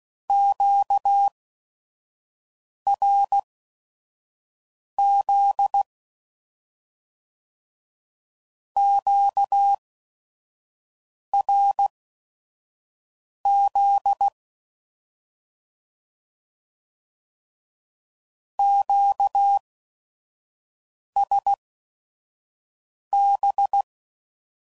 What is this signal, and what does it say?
Morse code, heard as a keyed tone, about 16 wpm, spaced as for 5 wpm, 800 Hz. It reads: QRZ QRZ QSB
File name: CW Code Groups @5WPM / Q Calls / Q Calls Twice